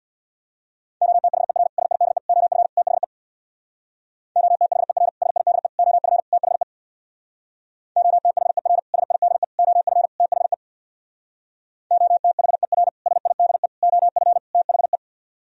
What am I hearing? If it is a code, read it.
Morse code, heard as a keyed tone, about 59 wpm, 700 Hz. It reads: OTHER SIDE OF THE OTHER SIDE OF THE OTHER SIDE OF THE OTHER SIDE OF THE
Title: Speed-Racing - Sets of 4 Words 50wpm